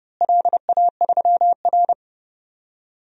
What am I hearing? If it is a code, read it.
Morse code, heard as a keyed tone, about 32 wpm, 700 Hz. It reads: LA3R
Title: DXCC-100 Call Signs - 4 Characters - Repeated 3x - 30wpm